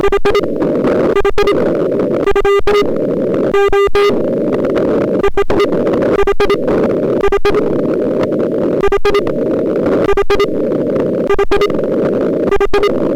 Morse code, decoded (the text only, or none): none